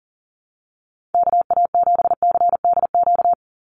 Morse code, heard as a keyed tone, about 40 wpm, 700 Hz. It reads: KA7CDQ
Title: USA Call Signs 40wpm